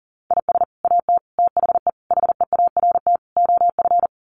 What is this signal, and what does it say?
Morse code, heard as a keyed tone, about 40 wpm, 700 Hz. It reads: IS AT THE HEART OF
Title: Sets of 5 Words Encore2 40wpm